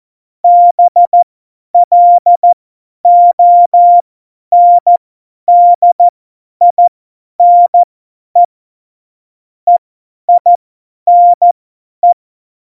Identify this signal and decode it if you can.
Morse code, heard as a keyed tone, about 14 wpm, spaced as for 10 wpm, 700 Hz. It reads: BLONDINE EINE
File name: Morsekurs / Witze / Witz 25 Juli 17, Tempo 50 F70